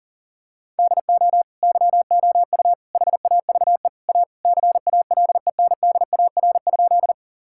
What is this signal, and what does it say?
Morse code, heard as a keyed tone, about 40 wpm, 700 Hz. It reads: DO YOU HAVE A CALENDAR?
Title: Sentences from Top 1600 Words 40wpm